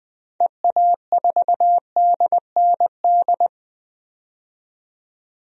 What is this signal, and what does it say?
Morse code, heard as a keyed tone, about 20 wpm, 700 Hz. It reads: EA4DND
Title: DXCC-100 Call Signs - 6 Characters - 20wpm